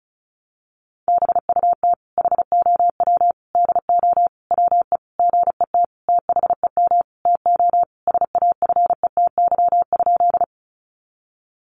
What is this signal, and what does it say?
Morse code, heard as a keyed tone, about 34 wpm, 700 Hz. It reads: BUT HOW DO WE GET THEM TO SAFETY?
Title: Sentences from Top 700 Words 35wpm